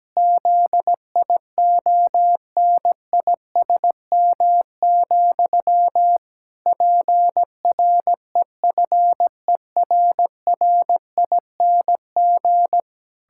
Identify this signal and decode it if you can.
Morse code, heard as a keyed tone, about 17 wpm, 700 Hz. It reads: ZIONISM, PREFERRING
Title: Pangram Challenge v2 - 17wpm